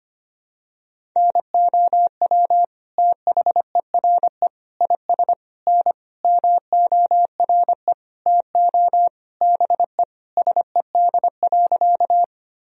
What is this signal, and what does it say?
Morse code, heard as a keyed tone, about 25 wpm, 700 Hz. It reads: NOW THERE IS N MORE TO BE SED.
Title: Sentences from Top 100 Words and Abbreviations 25wpm